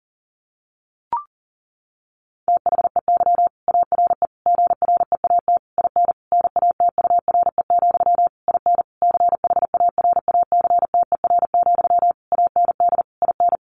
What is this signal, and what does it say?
Morse code, heard as a keyed tone, about 40 wpm, 700 Hz. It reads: THEY ARE GREAT IN NATURE, IN CHARACTER, AND IN
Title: Sentences from Top 1400 Words 40wpm